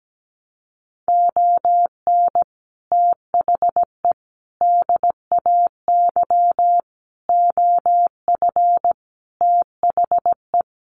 Morse code, heard as a keyed tone, about 17 wpm, 700 Hz. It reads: ON THE DAY OF THE
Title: Sets of 5 Words Encore 17wpm